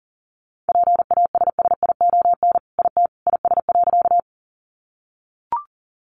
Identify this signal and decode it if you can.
Morse code, heard as a keyed tone, about 40 wpm, 700 Hz. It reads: PASSION IT IS.
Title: Sentences from Top 1300 Words 40wpm